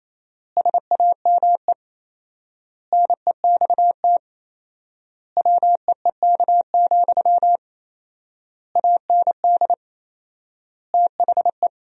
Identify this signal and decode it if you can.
Morse code, heard as a keyed tone, about 28 wpm, 700 Hz. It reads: SAME NEXT WEEK, AND THE
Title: 4x Word-Spacing - Sentences from Top 500 Words - 28wpm